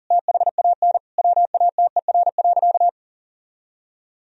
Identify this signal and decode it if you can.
Morse code, heard as a keyed tone, about 40 wpm, 700 Hz. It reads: THAN WATER.
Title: Sentences from Top 1000 Words - Review Part 4 - 40wpm